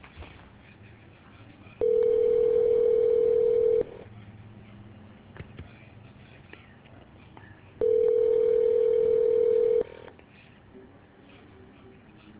Ringback tone (North America), 440 + 480 Hz.